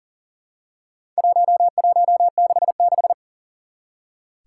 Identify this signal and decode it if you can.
Morse code, heard as a keyed tone, about 37 wpm, 700 Hz. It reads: 1166
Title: Four Challenging Letter-Number Combinations 40wpm